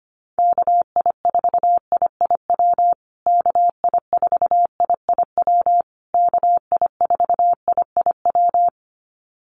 Morse code, heard as a keyed tone, about 25 wpm, 700 Hz. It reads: KI4IIW KI4IIW KI4IIW
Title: USA Call Signs Repeated 3x 25wpm